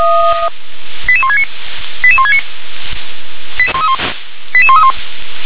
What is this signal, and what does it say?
Five-tone selective call (ZVEI), heard as quick runs of five tones, tones of about 70 ms. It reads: 88169 88169 88131
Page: /357/SDR/ZVEI/